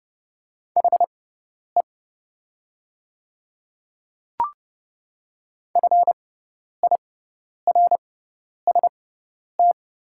Morse code, heard as a keyed tone, about 30 wpm, spaced as for 10 wpm, 700 Hz. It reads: HE FIRST
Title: 6x Character Spacing - Top 300 Words - 30wpm